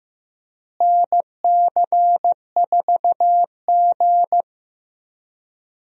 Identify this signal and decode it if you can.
Morse code, heard as a keyed tone, about 15 wpm, 700 Hz. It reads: NC4G
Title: USA Call Signs - 4 Characters - Repeated 2x - 15wpm